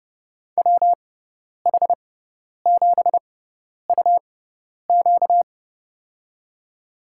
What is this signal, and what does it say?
Morse code, heard as a keyed tone, about 30 wpm, spaced as for 10 wpm, 700 Hz. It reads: WH7UQ
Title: USA Call Signs - 5 Characters - 6x Character Spacing - 30wpm